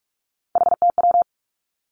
Morse code, heard as a keyed tone, about 46 wpm, 700 Hz. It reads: HTW
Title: Three Challenging Letter-Number Combinations 45wpm